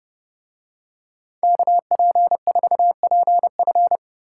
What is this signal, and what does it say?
Morse code, heard as a keyed tone, about 29 wpm, 700 Hz. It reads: KP4PF